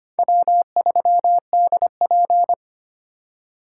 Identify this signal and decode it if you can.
Morse code, heard as a keyed tone, about 32 wpm, 700 Hz. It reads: W3DP